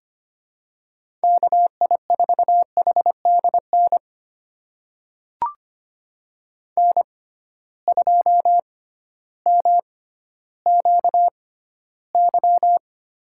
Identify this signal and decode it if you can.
Morse code, heard as a keyed tone, about 25 wpm, spaced as for 13 wpm, 700 Hz. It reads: KI4HDN N2MQY
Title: USA Call Signs - 6x Character Spacing - 25wpm